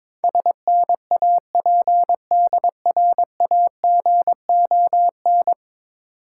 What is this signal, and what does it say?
Morse code, heard as a keyed tone, about 22 wpm, 690 Hz. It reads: SNAPDRAGON
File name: Difficult Words with Hints - 22wpm